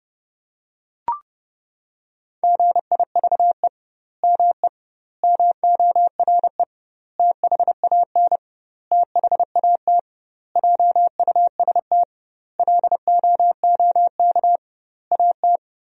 Morse code, heard as a keyed tone, about 30 wpm, 700 Hz. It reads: GIVE ME MORE THAN THAT JUST LOOK AT
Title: 2x Word-Spacing - Sentences from Top 200 Words - 30wpm